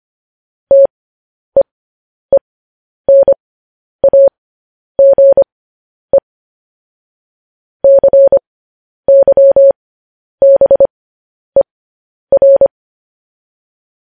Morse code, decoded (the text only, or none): TEENAGE CYBER